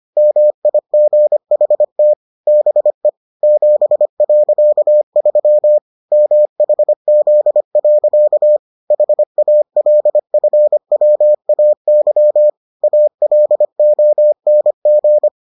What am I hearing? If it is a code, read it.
Morse code, heard as a keyed tone, about 25 wpm, 600 Hz. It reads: MIGHT BE 7.3 MHZ. HALFWAY ALONG